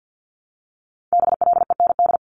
Morse code, heard as a keyed tone, about 50 wpm, 700 Hz. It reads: BLEND
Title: Multi-letter Phonemes 50wpm